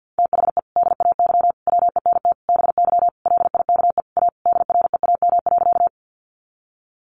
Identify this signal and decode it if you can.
Morse code, heard as a keyed tone, about 50 wpm, 700 Hz. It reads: THE DAY WENT BY LIKE A DREAM.
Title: Sentences from Top 1500 Words 50wpm